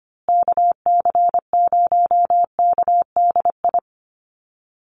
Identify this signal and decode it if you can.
Morse code, heard as a keyed tone, about 25 wpm, 700 Hz. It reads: KC0KDI
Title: USA Call Signs Repeated 3x 25wpm